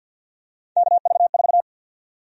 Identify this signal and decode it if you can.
Morse code, heard as a keyed tone, about 51 wpm, 700 Hz. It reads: KV4
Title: USA Call Sign Prefixes - 50wpm